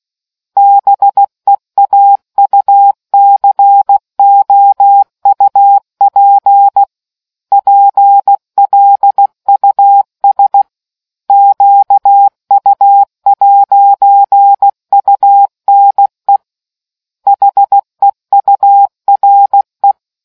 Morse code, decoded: BEAUCOUP PLUS QU'UNE HEURE

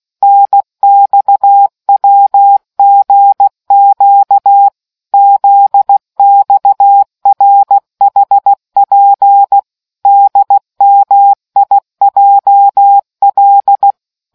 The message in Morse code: NXWGQ ZXRHP DMIJL

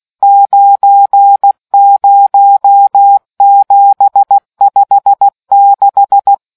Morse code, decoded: 90756

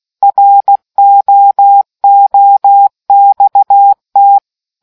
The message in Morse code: ROOXT

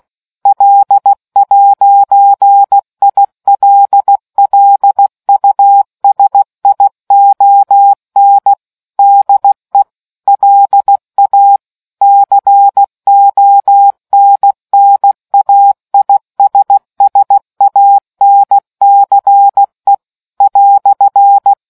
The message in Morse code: L'ILLUSION DE LA CONNAISSANCE "

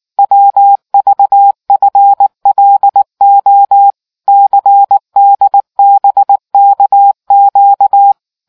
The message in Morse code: WVFLO CDBKQ

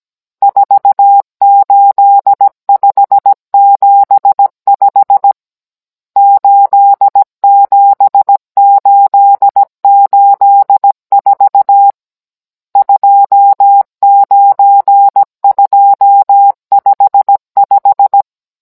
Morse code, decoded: 48575 87884 29255

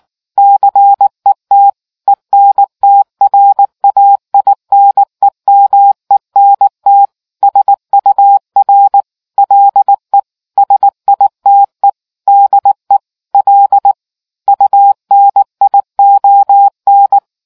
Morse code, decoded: CET ENTRAINEMENT SUR LE SITE DE L UNION

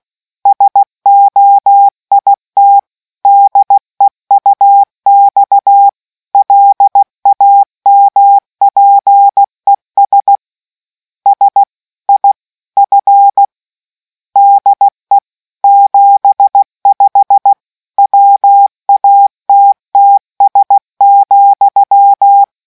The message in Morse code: SOIT DEUX LAMPES S I F DE 75 WATTS,

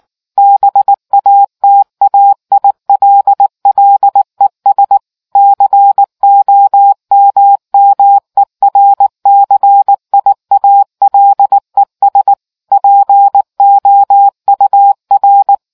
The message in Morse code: BATAILLES COMMERCIALES POUR

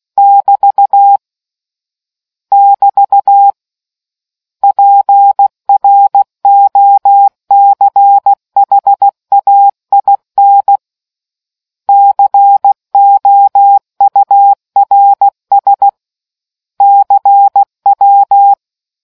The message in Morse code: <BT> <BT> PROCHAIN COURS CW

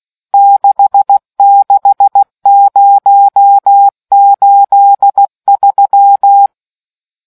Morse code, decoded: 66083

